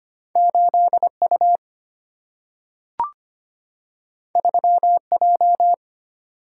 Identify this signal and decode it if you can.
Morse code, heard as a keyed tone, about 25 wpm, 700 Hz. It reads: 8U 3J
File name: Two Letter-Number Combinations - 25wpm